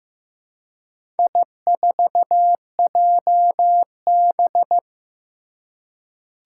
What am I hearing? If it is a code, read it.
Morse code, heard as a keyed tone, about 15 wpm, 700 Hz. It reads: I4JB
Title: Four Challenging Letter-Number Combinations 15wpm